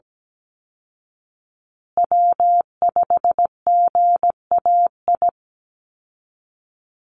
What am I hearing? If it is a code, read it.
Morse code, heard as a keyed tone, about 17 wpm, 700 Hz. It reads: W5GAI
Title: Call Signs Repeated 2x - 17wpm